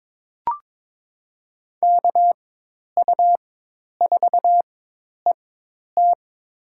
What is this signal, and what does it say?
Morse code, heard as a keyed tone, about 22 wpm, spaced as for 10 wpm, 700 Hz. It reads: KU4ET
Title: USA Call Signs - 4x Character Spacing - 22wpm